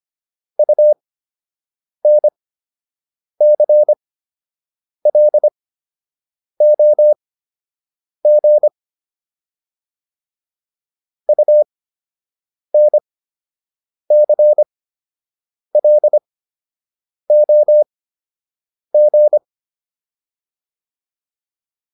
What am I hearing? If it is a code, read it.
Morse code, heard as a keyed tone, about 25 wpm, spaced as for 7 wpm, 600 Hz. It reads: UNCLOG UNCLOG